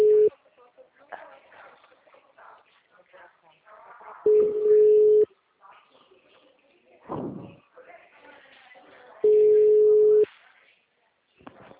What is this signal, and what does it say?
Ringback tone (Europe, 425 Hz), with speech.
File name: out- -2003- - - .369